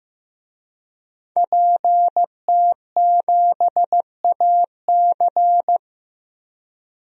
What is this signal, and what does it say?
Morse code, heard as a keyed tone, about 15 wpm, 700 Hz. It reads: PT7AC
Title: DXCC-100 Call Signs - 5 Characters - Repeated 3x - 15wpm